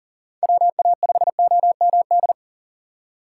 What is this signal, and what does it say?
Morse code, heard as a keyed tone, about 40 wpm, 700 Hz. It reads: WA5OMD
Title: USA Call Signs - 6 Characters - 4x Character Spacing - 40wpm